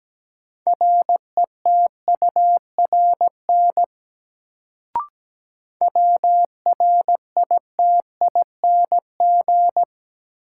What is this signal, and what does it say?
Morse code, heard as a keyed tone, about 17 wpm, 700 Hz. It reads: RETURN WRITING
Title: Sentences from Top 800 Words - New Word Drill - 17wpm